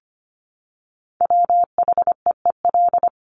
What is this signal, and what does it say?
Morse code, heard as a keyed tone, about 25 wpm, 700 Hz. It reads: WHEEL
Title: Top 500 Words Repeated 25wpm